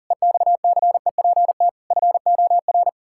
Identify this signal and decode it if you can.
Morse code, heard as a keyed tone, about 40 wpm, 700 Hz. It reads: EXCEPT FOR
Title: Pangram Challenge v2 - 40wpm